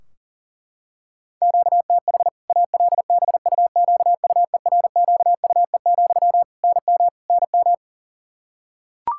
Morse code, heard as a keyed tone, about 40 wpm, 700 Hz. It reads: QTH ALBUQUERQUE, NM NM
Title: Practice QSO - Exchange 1 - Beginning - 40wpm